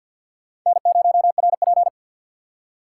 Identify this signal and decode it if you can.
Morse code, heard as a keyed tone, about 52 wpm, 700 Hz. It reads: N0RP
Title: USA Call Signs - 4 Characters - 3x Character Spacing - 50wpm